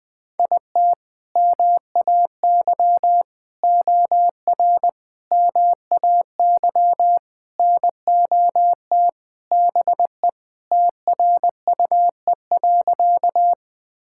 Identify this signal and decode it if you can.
Morse code, heard as a keyed tone, about 21 wpm, 700 Hz. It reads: IT MAY OR MAY NOT BE TRUE.
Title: Sentences with 5-Word Phrases - 20wpm